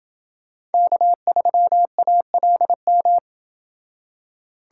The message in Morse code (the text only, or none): K3ALM